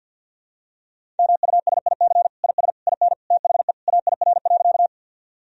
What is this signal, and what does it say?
Morse code, heard as a keyed tone, about 52 wpm, 700 Hz. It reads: MUSIC IS IN THE AIR.
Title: Sentences from Top 1000 Words - Review Part 1 - 50wpm